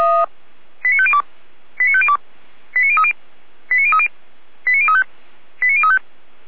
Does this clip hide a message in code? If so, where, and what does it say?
Yes: five-tone selective call (ZVEI) from 0.8 to 6.0 s, heard as quick runs of five tones, tones of about 70 ms. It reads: 89502 89502 89030 89030 89035 89035